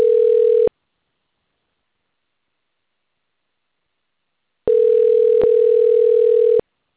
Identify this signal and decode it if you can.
Ringback tone (North America), 440 + 480 Hz.
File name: out- - - - - .59198